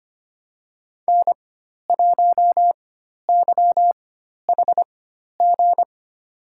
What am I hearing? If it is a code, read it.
Morse code, heard as a keyed tone, about 25 wpm, spaced as for 12 wpm, 700 Hz. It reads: N1YHG
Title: USA Call Signs - 5 Characters - 4x Character Spacing - 25wpm